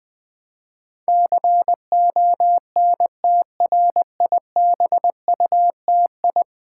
Morse code, heard as a keyed tone, about 20 wpm, 700 Hz. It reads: CONTRIBUTI
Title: Build A Word - 5 Letters At A Time - 12-Letter Words - 20wpm